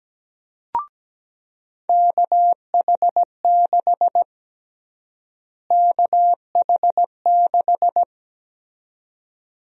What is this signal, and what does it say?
Morse code, heard as a keyed tone, about 17 wpm, 700 Hz. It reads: KH6 KH6
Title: USA Call Sign Prefixes - Repeated 2x - 17wpm